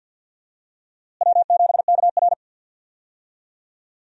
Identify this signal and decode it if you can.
Morse code, heard as a keyed tone, about 48 wpm, 700 Hz. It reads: W7KR
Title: USA Call Signs - 4 Characters - Repeated 2x - 50wpm